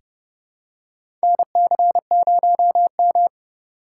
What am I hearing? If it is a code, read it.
Morse code, heard as a keyed tone, about 30 wpm, 700 Hz. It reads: NC0M
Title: USA Call Signs - 4 Characters - 8x Character Spacing - 30wpm